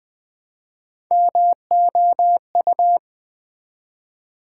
Morse code, heard as a keyed tone, about 20 wpm, 700 Hz. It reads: MOU
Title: Build A Word - 1 Letter At A Time - 5-Letter Words - 20wpm